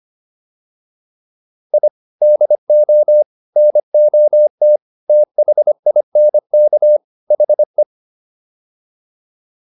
Morse code, heard as a keyed tone, about 25 wpm, 600 Hz. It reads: I DO NOT THINK HE